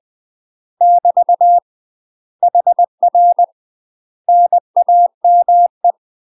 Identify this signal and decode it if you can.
Morse code, heard as a keyed tone, about 20 wpm, 700 Hz. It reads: <BT> HR NAME
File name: UK MASTER QSOs TRAINING / 20WPM / UK MASTER QSOs 0002